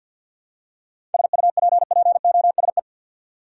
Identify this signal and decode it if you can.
Morse code, heard as a keyed tone, about 48 wpm, 700 Hz. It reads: SUPPOSE
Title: Seven Letter Words - 50wpm